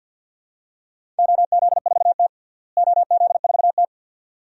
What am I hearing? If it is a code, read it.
Morse code, heard as a keyed tone, about 46 wpm, 700 Hz. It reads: OZ4T OZ4T
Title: DXCC-100 Call Signs - 4 Characters - Repeated 2x - 50wpm